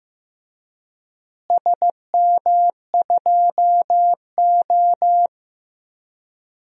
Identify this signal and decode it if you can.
Morse code, heard as a keyed tone, about 15 wpm, 700 Hz. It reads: SM2O
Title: DXCC-100 Call Signs - 4 Characters - Repeated 3x - 15wpm